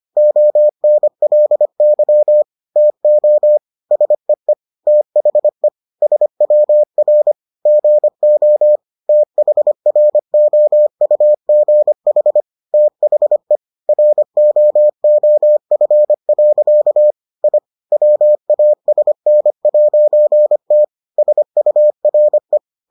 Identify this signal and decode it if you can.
Morse code, heard as a keyed tone, about 25 wpm, 600 Hz. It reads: ONLY TO SEE THE SWR GO THROUGH THE ROOF. I WASN'T SURE